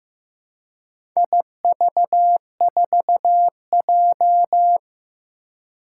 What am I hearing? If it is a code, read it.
Morse code, heard as a keyed tone, about 15 wpm, 700 Hz. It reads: IV4J
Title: Four Challenging Letter-Number Combinations 15wpm